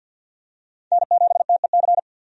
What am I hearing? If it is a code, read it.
Morse code, heard as a keyed tone, about 51 wpm, 700 Hz. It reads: N7TEC
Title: USA Call Signs - 5 Characters - 50wpm